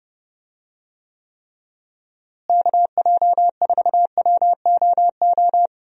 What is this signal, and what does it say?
Morse code, heard as a keyed tone, about 30 wpm, 700 Hz. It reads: KJ4WOO
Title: USA Call Signs - 6 Characters - Repeated 3x - 30wpm